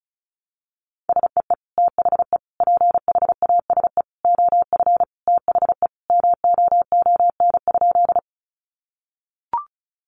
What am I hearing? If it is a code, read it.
Morse code, heard as a keyed tone, about 34 wpm, 700 Hz. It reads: SEE THE PHASE OF THE MOON?
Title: Sentences from Top 1200 Words 35wpm